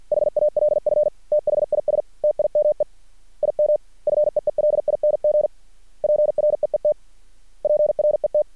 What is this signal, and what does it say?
Morse code, heard as a keyed tone, about 50 wpm, 600 Hz. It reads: HALF THIS TIME IM FEELING PREET PRET
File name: archived / CRSnet